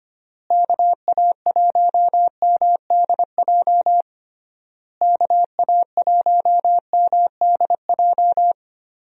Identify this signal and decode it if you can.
Morse code, heard as a keyed tone, about 25 wpm, 700 Hz. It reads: KA1MDJ KA1MDJ
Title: USA Call Signs - 6 Characters - Repeated 3x - 25wpm